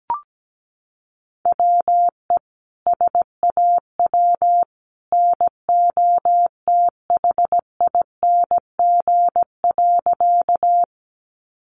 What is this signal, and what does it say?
Morse code, heard as a keyed tone, about 17 wpm, 700 Hz. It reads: WE SAW NOTHING.
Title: A Sound Of Thunder 17wpm